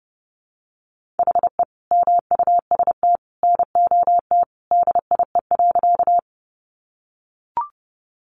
Morse code, heard as a keyed tone, about 30 wpm, 700 Hz. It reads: HE MUST NOT DIE.
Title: Sentences from Top 1300 Words 30wpm